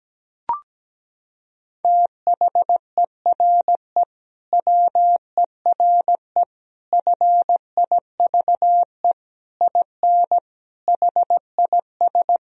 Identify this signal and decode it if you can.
Morse code, heard as a keyed tone, about 17 wpm, 700 Hz. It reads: THERE WERE FIVE IN HIS
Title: Sentences from Top 1000 Words - Review Part 2 - 17wpm